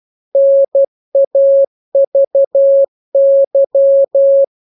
Morse code, heard as a keyed tone, about 12 wpm, 550 Hz. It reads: NAVY
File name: jokes / joke 2 12 0000